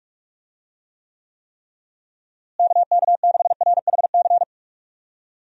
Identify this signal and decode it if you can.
Morse code, heard as a keyed tone, about 46 wpm, 700 Hz. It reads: KK6RHC
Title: USA Call Signs - 6 Characters - Repeated 3x - 45wpm